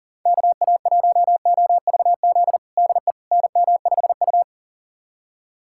Morse code, heard as a keyed tone, about 40 wpm, 700 Hz. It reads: KA1OVZ DE NM5U
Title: USA Callsign DE Callsign 40wpm